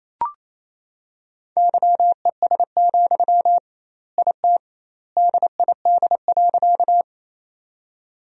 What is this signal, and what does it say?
Morse code, heard as a keyed tone, about 27 wpm, 700 Hz. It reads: YES, IT DID.
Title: 2x Word-Spacing - Sentences from Top 500 Words - 28wpm